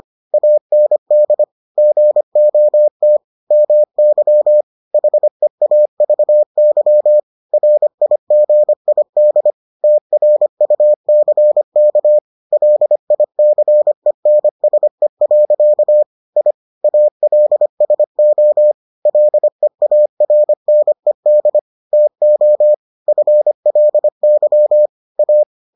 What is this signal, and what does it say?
Morse code, heard as a keyed tone, about 25 wpm, 600 Hz. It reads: AND GOT MY HEAVY RIGID TRUCK LICENSE. I ALSO LEARNED TO FLY A